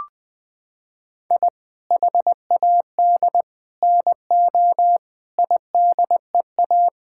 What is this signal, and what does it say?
Morse code, heard as a keyed tone, about 21 wpm, 700 Hz. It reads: I HAD NO IDEA
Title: Sets of 8 Words - 20wpm